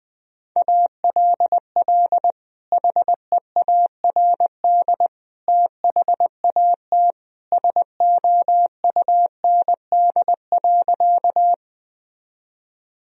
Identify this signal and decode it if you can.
Morse code, heard as a keyed tone, about 20 wpm, 700 Hz. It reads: ALL HEARD THAT SOUND.
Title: Sentences from Top 1000 Words - Review Part 1 - 20wpm